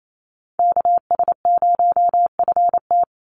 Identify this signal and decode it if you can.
Morse code, heard as a keyed tone, about 28 wpm, 700 Hz. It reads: KS0FT
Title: USA Call Signs 28wpm